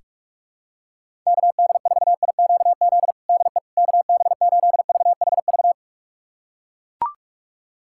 Morse code, heard as a keyed tone, about 44 wpm, 690 Hz. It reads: KD4IQZ DE KB8VHV
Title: USA Callsign DE Callsign - 6 Characters - 45wpm